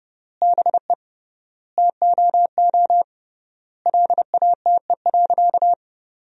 Morse code, heard as a keyed tone, about 30 wpm, 700 Hz. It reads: BE TOO LATE.
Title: 3x Word-Spacing - Sentences from Top 300 Words - 30wpm